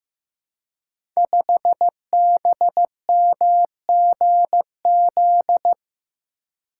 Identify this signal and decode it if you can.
Morse code, heard as a keyed tone, about 15 wpm, 700 Hz. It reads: 5BMGZ
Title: Code Groups - 5 Letter:Numbers - 15wpm